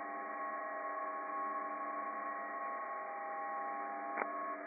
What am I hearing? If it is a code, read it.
PSK31, about 31.25 baud.
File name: some psk31 on 40m